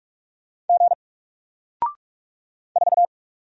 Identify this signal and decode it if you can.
Morse code, heard as a keyed tone, about 44 wpm, 700 Hz. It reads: G 4
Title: Single Letter-Number 45wpm